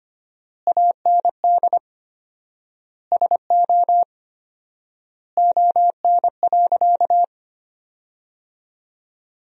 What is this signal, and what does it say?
Morse code, heard as a keyed tone, about 25 wpm, 700 Hz. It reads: AND SO ON.
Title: 4x Word-Spacing - Sentences from Top 100 Words - 25wpm